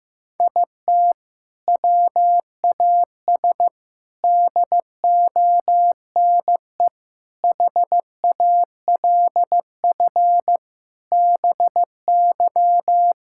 Morse code, heard as a keyed tone, about 16 wpm, 700 Hz. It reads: IT WAS DONE HALF BY
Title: Sentences from Top 1000 Words - Review Part 5 - 15wpm